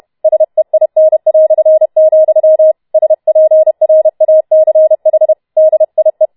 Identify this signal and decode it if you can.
Morse code, heard as a keyed tone, about 31 wpm, 630 Hz. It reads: SEIN", SPRACH DIE
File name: Schneewittchen Teil4 32wpm